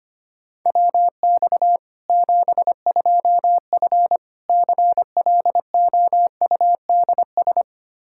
Morse code, heard as a keyed tone, about 25 wpm, 700 Hz. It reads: WX 72F CLOUDS
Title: K1USN SST QSO Practice 25wpm